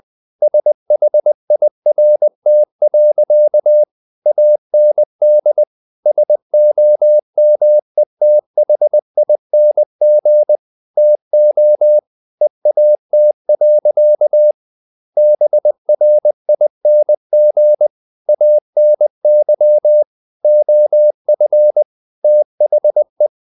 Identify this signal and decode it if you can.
Morse code, heard as a keyed tone, about 20 wpm, 600 Hz. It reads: SHIRT. AND SOMETHING TO EAT. BRING ANY OF THE
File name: the old man and the sea part 2 20wpm